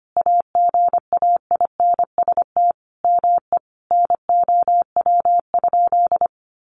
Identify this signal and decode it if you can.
Morse code, heard as a keyed tone, about 25 wpm, 700 Hz. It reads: AGAINST ME NOW?
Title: Sentences from Top 400 Words 25wpm